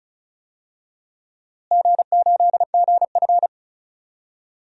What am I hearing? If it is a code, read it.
Morse code, heard as a keyed tone, about 35 wpm, 700 Hz. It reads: G8GF
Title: DXCC-100 Call Signs - 4 Characters - Repeated 2x - 35wpm